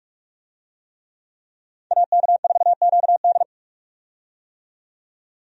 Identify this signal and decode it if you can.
Morse code, heard as a keyed tone, about 46 wpm, 700 Hz. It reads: AK4QD
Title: USA Call Signs - 5 Characters - 45wpm